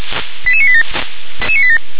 Five-tone selective call (ZVEI), heard as quick runs of five tones, tones of about 70 ms. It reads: 88987